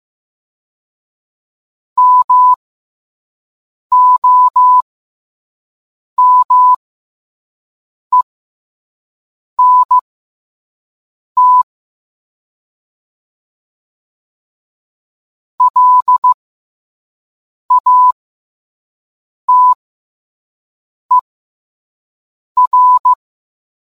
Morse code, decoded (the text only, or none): MOMENT LATER